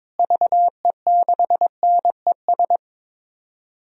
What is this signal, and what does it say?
Morse code, heard as a keyed tone, about 22 wpm, 700 Hz. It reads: VE6NES
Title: DXCC-100 Call Signs - 6 Characters - Repeated 3x - 22wpm